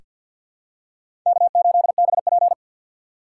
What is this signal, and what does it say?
Morse code, heard as a keyed tone, about 46 wpm, 700 Hz. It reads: K8BP
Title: USA Call Signs - 3x Character Spacing - 50wpm